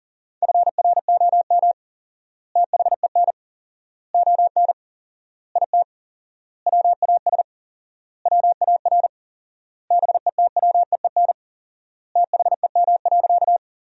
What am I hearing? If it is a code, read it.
Morse code, heard as a keyed tone, about 40 wpm, 700 Hz. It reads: FROM THEN ON IT WAS WAR BETWEEN THEM.
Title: 4x Word-Spacing - Sentences from Top 400 Words - 40wpm